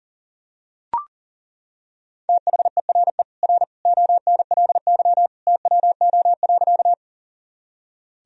Morse code, heard as a keyed tone, about 40 wpm, 700 Hz. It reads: THERE R ONLY TWO.
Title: Sentences from Top 200 Words and Abbreviations 40wpm